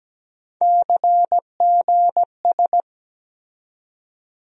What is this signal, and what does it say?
Morse code, heard as a keyed tone, about 17 wpm, 700 Hz. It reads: CGS
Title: USA Call Sign Postfixes - 3 Letters 17wpm